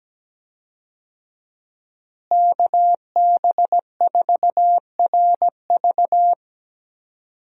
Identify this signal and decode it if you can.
Morse code, heard as a keyed tone, about 17 wpm, 700 Hz. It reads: KB4RV
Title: USA Call Signs - 5 Characters - Repeated 3x - 17wpm